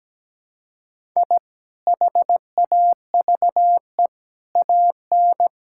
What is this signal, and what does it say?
Morse code, heard as a keyed tone, about 17 wpm, 700 Hz. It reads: I HAVE AN
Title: Sentences from Top 1000 Words - Review Part 5 - 17wpm